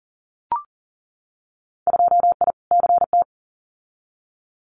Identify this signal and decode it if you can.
Morse code, heard as a keyed tone, about 38 wpm, 700 Hz. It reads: 2I CT
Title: Winter Field Day Exchanges 40wpm